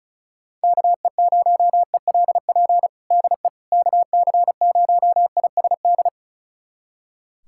Morse code, heard as a keyed tone, about 36 wpm, 700 Hz. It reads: KE0ELP DE KC0ISD
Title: USA Callsign DE Callsign - 6 Characters - 35wpm